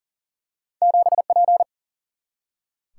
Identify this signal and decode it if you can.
Morse code, heard as a keyed tone, about 39 wpm, 700 Hz. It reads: 7P
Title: Single-Letter Single-Number 40wpm